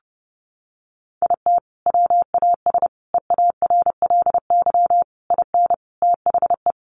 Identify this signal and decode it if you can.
Morse code, heard as a keyed tone, about 30 wpm, 700 Hz. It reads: IT WAS EARLY IN THE
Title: Sentences from Top 1200 Words 30wpm